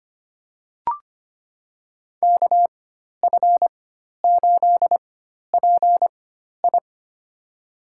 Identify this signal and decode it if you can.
Morse code, heard as a keyed tone, about 25 wpm, spaced as for 12 wpm, 700 Hz. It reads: KF8PI
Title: USA Call Signs - 5 Characters - 4x Character Spacing - 25wpm